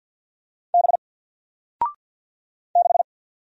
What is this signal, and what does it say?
Morse code, heard as a keyed tone, about 50 wpm, 700 Hz. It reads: B 6
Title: Single Character (Letter:Number:Punctuation:Prosign) 50wpm